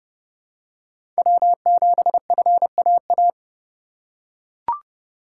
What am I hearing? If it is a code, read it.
Morse code, heard as a keyed tone, about 30 wpm, 700 Hz. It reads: W7FAA
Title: USA Call Signs - 5 Characters - 4x Character Spacing - 30wpm